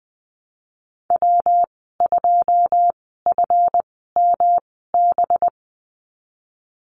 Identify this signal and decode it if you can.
Morse code, heard as a keyed tone, about 20 wpm, spaced as for 15 wpm, 700 Hz. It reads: W2FMB
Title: 186 Lesson 30g - Letter b - Call Signs 20wpm